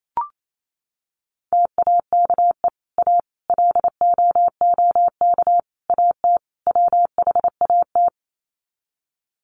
Morse code, heard as a keyed tone, about 28 wpm, 700 Hz. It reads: TAKE A LOOK AT WHAT
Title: Sets of 5 Words Encore2 28wpm